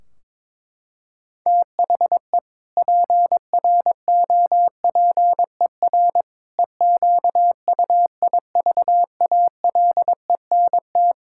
Morse code, heard as a keyed tone, about 22 wpm, 700 Hz. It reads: THE PROPER EQUIVALENT
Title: Sentences from Top 1600 Words 22wpm